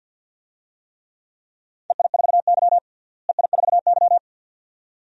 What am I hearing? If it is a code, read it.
Morse code, heard as a keyed tone, about 50 wpm, 700 Hz. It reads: EI4Y EI4Y
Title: DXCC-100 Call Signs - 4 Characters - Repeated 2x - 50wpm